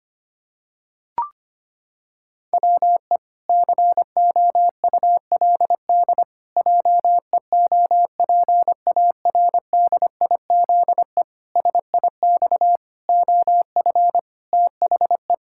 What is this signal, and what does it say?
Morse code, heard as a keyed tone, about 25 wpm, 700 Hz. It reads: WE COULD JEOPARDIZE SIX OF THE
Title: Pangram Challenge v2 - 25wpm